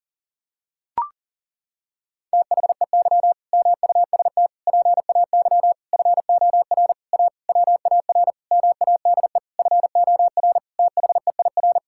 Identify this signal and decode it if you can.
Morse code, heard as a keyed tone, about 40 wpm, 700 Hz. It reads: THEY MUST PAY FOR A WAR MADE FOR THEIR
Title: Sentences from Top 1000 Words - Review Part 6 - 40wpm